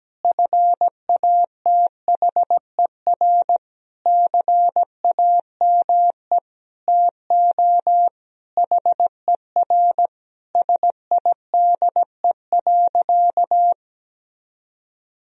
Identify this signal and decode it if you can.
Morse code, heard as a keyed tone, about 17 wpm, 700 Hz. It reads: FATHER CAME TO HER SIDE.
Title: Sentences from Top 1000 Words - Review Part 1 - 17wpm